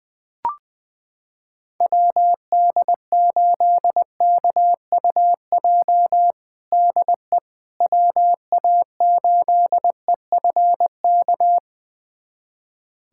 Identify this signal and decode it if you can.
Morse code, heard as a keyed tone, about 21 wpm, 700 Hz. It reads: WD8KUJ DE WA8EFK
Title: USA Callsign DE Callsign 20wpm